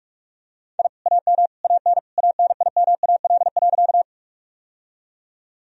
Morse code, heard as a keyed tone, about 42 wpm, 700 Hz. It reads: I AM AN ANIMAL.
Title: Sentences from Top 1000 Words - Review Part 1 - 45wpm